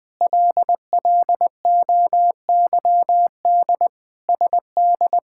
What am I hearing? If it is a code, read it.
Morse code, heard as a keyed tone, about 20 wpm, 700 Hz. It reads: LLOYD SD
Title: K1USN SST QSO Practice - 20wpm